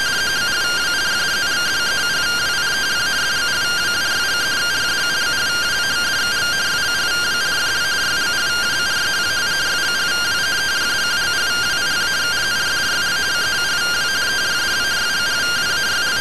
Two-tone teleprinter signal, about 170 Hz shift.